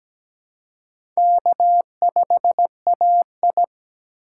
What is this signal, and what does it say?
Morse code, heard as a keyed tone, about 17 wpm, 700 Hz. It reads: K5AI
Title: USA Call Signs - 4 Characters - 3x Character Spacing - 17wpm